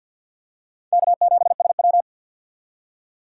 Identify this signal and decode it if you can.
Morse code, heard as a keyed tone, about 51 wpm, 700 Hz. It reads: K7SW
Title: USA Call Signs - 6x Character Spacing - 50wpm